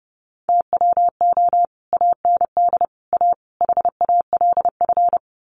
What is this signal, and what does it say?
Morse code, heard as a keyed tone, about 30 wpm, 700 Hz. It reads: TWO AND A HALF
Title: Sets of 4 Words 30wpm